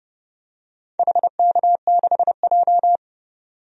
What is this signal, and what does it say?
Morse code, heard as a keyed tone, about 30 wpm, 700 Hz. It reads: HK6J
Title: DXCC-100 Call Signs - 4 Characters - 30wpm